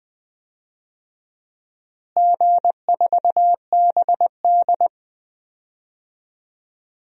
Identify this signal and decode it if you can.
Morse code, heard as a keyed tone, about 20 wpm, 700 Hz. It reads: G4BD
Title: DXCC-100 Call Signs - 4 Characters - Repeated 3x - 20wpm